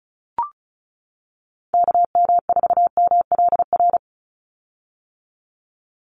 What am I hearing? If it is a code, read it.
Morse code, heard as a keyed tone, about 34 wpm, 700 Hz. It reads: KM4MLR
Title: USA Call Signs 35wpm